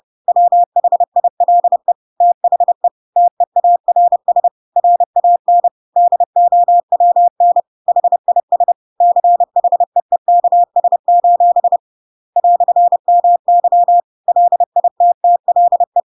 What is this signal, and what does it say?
Morse code, heard as a keyed tone, about 30 wpm, 700 Hz. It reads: WHILE THE TEARS RAN DOWN HIS CHEEKS: "MY LITTLE
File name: Pinocchio / Pinocchio ch7 30 30 ik1hge